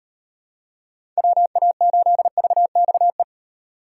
Morse code, heard as a keyed tone, about 38 wpm, 700 Hz. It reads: WA8VXE